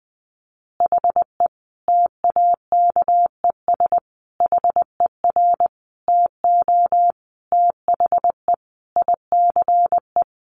Morse code, heard as a keyed tone, about 20 wpm, 700 Hz. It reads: HE TAKES HER TO THE ICE
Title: Additional Easy Phrases 20wpm